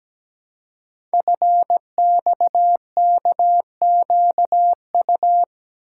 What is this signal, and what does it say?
Morse code, heard as a keyed tone, about 17 wpm, 700 Hz. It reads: FXKQU
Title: Code Groups - 5 Letters - 17wpm